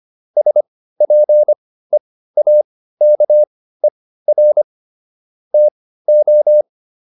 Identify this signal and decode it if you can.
Morse code, heard as a keyed tone, about 25 wpm, spaced as for 15 wpm, 600 Hz. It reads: SPEAKER TO
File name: SS203 15